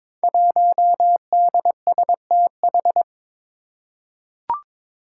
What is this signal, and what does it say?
Morse code, heard as a keyed tone, about 22 wpm, 700 Hz. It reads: 1DSTH
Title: Five Challenging Letter-Number Combinations 22wpm